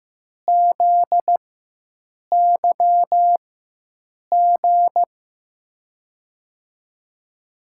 Morse code, heard as a keyed tone, about 15 wpm, spaced as for 7 wpm, 700 Hz. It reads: ZYG
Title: USA Call Signs - 6 Characters - 4x Character Spacing - 15wpm